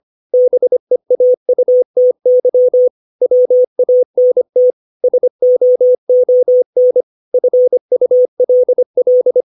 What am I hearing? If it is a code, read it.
Morse code, heard as a keyed tone, about 25 wpm, 500 Hz. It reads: BEAUTY WANT SOON FULL
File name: 632 words / 632 Words 25WPM